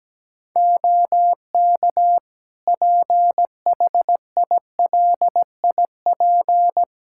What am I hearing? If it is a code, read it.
Morse code, heard as a keyed tone, about 17 wpm, 700 Hz. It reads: OK PHILIP
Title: K1USN SST QSO Practice 17wpm